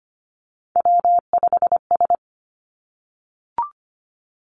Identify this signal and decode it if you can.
Morse code, heard as a keyed tone, about 25 wpm, 700 Hz. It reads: W5S
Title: Three Challenging Letter-Number Combinations 25wpm